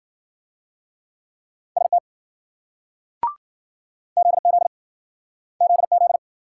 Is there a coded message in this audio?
Yes: Morse code from 1.8 to 6.2 s, heard as a keyed tone, about 54 wpm, 700 Hz. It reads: ST ZZ ZZ